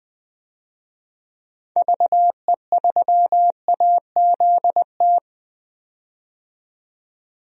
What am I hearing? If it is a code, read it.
Morse code, heard as a keyed tone, about 20 wpm, 700 Hz. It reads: VE3AZT
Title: DXCC-100 Call Signs - 6 Characters - 20wpm